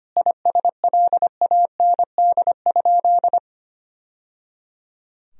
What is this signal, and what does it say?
Morse code, heard as a keyed tone, about 25 wpm, 700 Hz. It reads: ISLAND?
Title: Sentences from Top 1000 Words - Review Part 3 - 25wpm